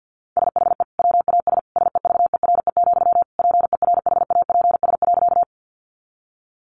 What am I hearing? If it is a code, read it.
Morse code, heard as a keyed tone, about 50 wpm, 700 Hz. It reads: SHE WAS SEVERE, PERHAPS.
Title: Sentences from Top 1400 Words 50wpm